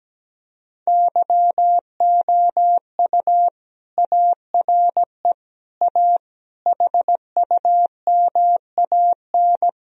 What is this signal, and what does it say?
Morse code, heard as a keyed tone, about 17 wpm, 700 Hz. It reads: YOU ARE A HUMAN
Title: Sentences from Top 1000 Words - Review Part 3 - 17wpm